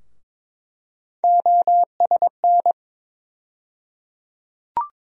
Morse code, heard as a keyed tone, about 22 wpm, 700 Hz. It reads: OSN
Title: USA Call Sign Postfixes - 3 Letters 22wpm